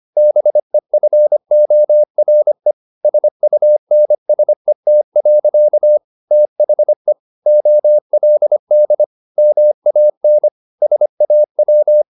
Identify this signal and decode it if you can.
Morse code, heard as a keyed tone, about 25 wpm, 600 Hz. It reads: BEFORE SUNSET. THE OLD MAN SAW